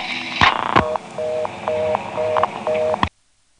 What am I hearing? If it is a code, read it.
Reorder tone (North America), 480 + 620 Hz, with noise.
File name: phonetrips / Kenly-ClaytonNC2HQ